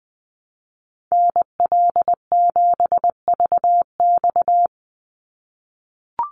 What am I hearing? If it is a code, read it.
Morse code, heard as a keyed tone, about 20 wpm, 700 Hz. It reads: NL7VX
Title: Call Signs - 20wpm